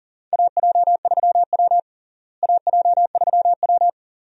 Morse code, heard as a keyed tone, about 38 wpm, 700 Hz. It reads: AJ3W AJ3W
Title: USA Call Signs - 4 Characters - Repeated 2x - 40wpm